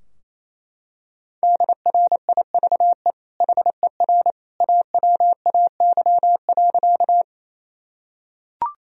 Morse code, heard as a keyed tone, about 28 wpm, 700 Hz. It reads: DRIVE HER AWAY.
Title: Sentences from Top 1000 Words - Review Part 2 - 28wpm